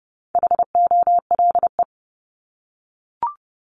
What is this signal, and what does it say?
Morse code, heard as a keyed tone, about 30 wpm, 700 Hz. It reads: HOLE
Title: Top 1501-1750 Words 30wpm